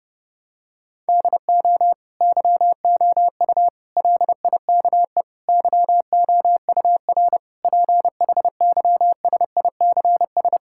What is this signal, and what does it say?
Morse code, heard as a keyed tone, about 32 wpm, 700 Hz. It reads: DO YOU LIKE YOUR PHYSICS
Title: Sentences from Top 1000 Words - Review Part 5 - 30wpm